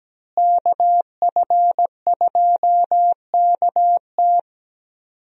Morse code, heard as a keyed tone, about 17 wpm, 700 Hz. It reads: KF2KT
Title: USA Call Signs - 5 Characters - Repeated 2x - 17wpm